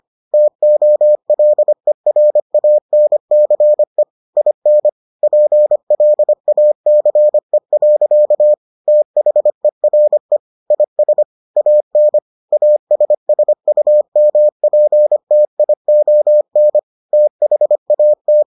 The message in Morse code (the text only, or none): TOLERANCE IN PLACE. THERE IS AN ASSUMPTION THAT